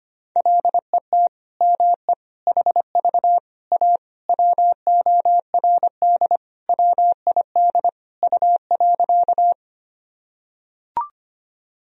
Morse code, heard as a keyed tone, about 25 wpm, 700 Hz. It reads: LET ME HV A WORD WID U.
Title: Sentences from Top 300 Words and Abbreviations 25wpm